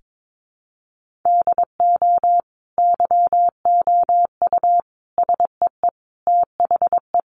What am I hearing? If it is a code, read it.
Morse code, heard as a keyed tone, about 22 wpm, 700 Hz. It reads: DO YOU SEE THE
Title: Sentences from Top 1500 Words 22wpm